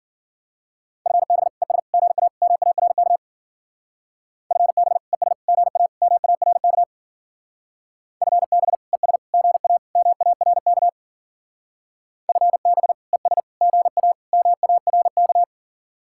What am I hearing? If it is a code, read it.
Morse code, heard as a keyed tone, about 48 wpm, 700 Hz. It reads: FB ES GA MARK FB ES GA MARK FB ES GA MARK FB ES GA MARK
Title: Speed-Racing Common QSO Phrases - 40wpm